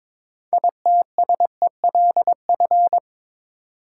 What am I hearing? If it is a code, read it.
Morse code, heard as a keyed tone, about 22 wpm, 700 Hz. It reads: ITSELF
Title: Sentences from Top 800 Words - New Word Drill - 22wpm